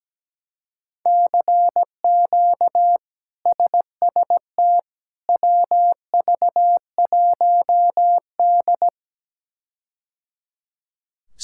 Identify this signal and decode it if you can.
Morse code, heard as a keyed tone, about 17 wpm, 700 Hz. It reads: CQ SST WV1D
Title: K1USN SST QSO Practice - 17wpm